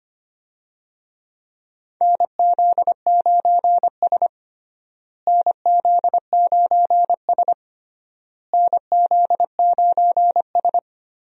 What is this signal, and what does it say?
Morse code, heard as a keyed tone, about 25 wpm, 700 Hz. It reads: NZ9S NZ9S NZ9S
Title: USA Call Signs - 4 Characters - Repeated 3x - 25wpm